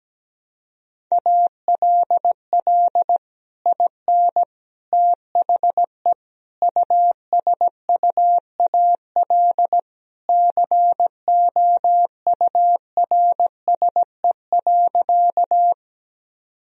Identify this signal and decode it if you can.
Morse code, heard as a keyed tone, about 17 wpm, 700 Hz. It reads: ALL IN THE USUAL COURSE.
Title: Sentences from Top 1000 Words - Review Part 2 - 17wpm